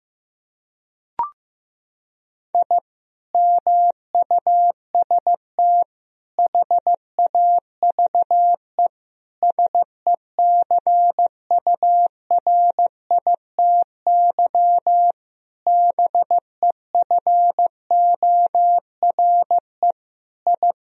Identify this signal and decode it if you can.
Morse code, heard as a keyed tone, about 15 wpm, 700 Hz. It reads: I MUST HAVE SECURITY BEFORE I
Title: Sentences from Top 1000 Words - Review Part 4 - 15wpm